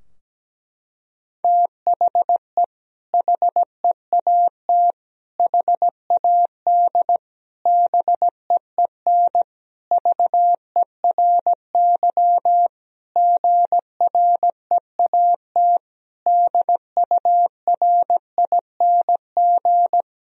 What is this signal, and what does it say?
Morse code, heard as a keyed tone, about 17 wpm, 700 Hz. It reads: THE HEAT HAD BEEN VERY GREAT DURING
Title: Sentences from Top 1000 Words - Review Part 3 - 17wpm